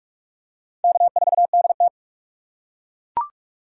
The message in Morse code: K4DT

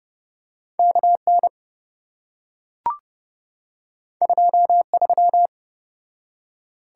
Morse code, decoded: KN 23

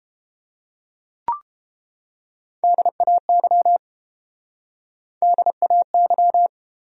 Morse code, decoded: DAY DAY

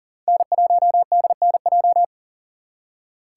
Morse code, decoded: N1DNJ

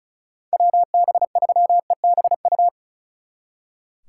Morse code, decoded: WB3EBU

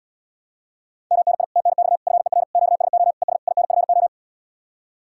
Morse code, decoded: MNI TNX FER QSO ES INFO

